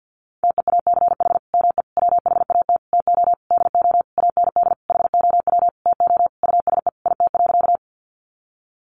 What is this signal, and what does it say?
Morse code, heard as a keyed tone, about 50 wpm, 700 Hz. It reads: TEACH ME WHAT TO DO AND HOW TO USE IT.
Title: Sentences from Top 500 Words 50wpm